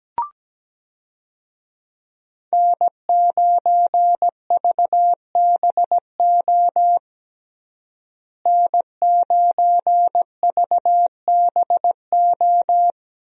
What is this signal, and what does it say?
Morse code, heard as a keyed tone, about 17 wpm, 700 Hz. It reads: N9VBO N9VBO
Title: USA Call Signs - 5 Characters - Repeated 3x - 17wpm